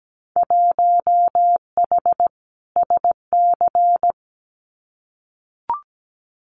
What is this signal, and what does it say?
Morse code, heard as a keyed tone, about 17 wpm, 700 Hz. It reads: 1H SC
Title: Winter Field Day Exchanges 17wpm